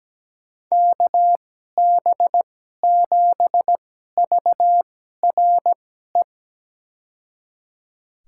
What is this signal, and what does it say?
Morse code, heard as a keyed tone, about 17 wpm, spaced as for 12 wpm, 700 Hz. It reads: KB7VRE
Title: USA Call Signs - 6 Characters - 2x Character Spacing - 17wpm